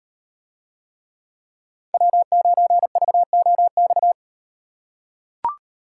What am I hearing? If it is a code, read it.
Morse code, heard as a keyed tone, about 38 wpm, 700 Hz. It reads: W9VOX